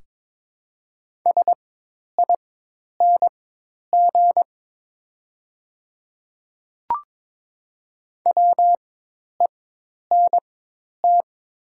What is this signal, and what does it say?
Morse code, heard as a keyed tone, about 23 wpm, spaced as for 10 wpm, 700 Hz. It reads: SING WENT
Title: 4x Character Spacing - Top 500 Words - 22wpm